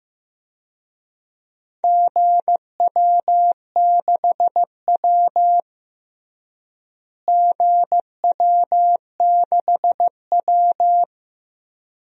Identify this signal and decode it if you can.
Morse code, heard as a keyed tone, about 16 wpm, 700 Hz. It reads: GW6W GW6W
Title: DXCC-100 Call Signs - 4 Characters - Repeated 2x - 15wpm